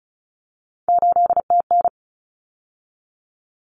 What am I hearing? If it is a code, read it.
Morse code, heard as a keyed tone, about 34 wpm, 700 Hz. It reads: 8TN
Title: Common Cut Numbers - 3-Digit 35wpm